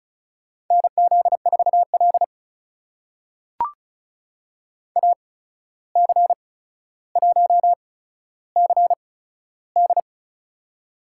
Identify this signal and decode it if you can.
Morse code, heard as a keyed tone, about 36 wpm, spaced as for 14 wpm, 700 Hz. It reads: NZ4L AC1CD
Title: USA Call Signs - 8x Character Spacing - 35wpm